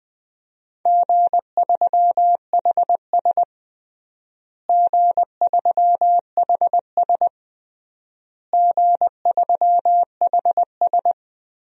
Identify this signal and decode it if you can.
Morse code, heard as a keyed tone, about 20 wpm, 700 Hz. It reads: G3HS G3HS G3HS
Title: DXCC-100 Call Signs - 4 Characters - Repeated 3x - 20wpm